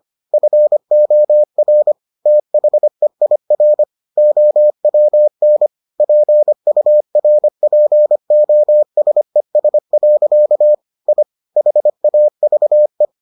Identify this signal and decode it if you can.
Morse code, heard as a keyed tone, about 25 wpm, 600 Hz. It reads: FOR THEIR OWN PURPOSES. I HAVE